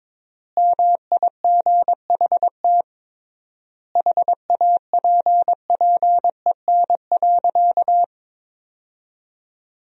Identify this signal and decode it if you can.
Morse code, heard as a keyed tone, about 22 wpm, 700 Hz. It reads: MIGHT HAPPEN.
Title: 3x Word-Spacing - Sentences from Top 400 Words - 22wpm